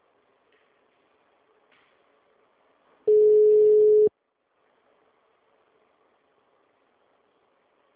Ringback tone (Europe, 425 Hz).